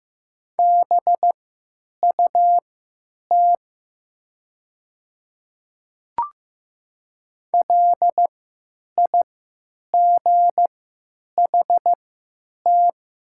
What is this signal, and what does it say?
Morse code, heard as a keyed tone, about 15 wpm, spaced as for 9 wpm, 700 Hz. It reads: BUT LIGHT
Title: 3x Character Spacing - Top 200 Words - 15wpm